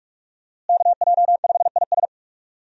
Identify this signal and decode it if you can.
Morse code, heard as a keyed tone, about 46 wpm, 700 Hz. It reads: KJ5IS
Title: USA Call Signs - 5 Characters - Repeated 2x - 45wpm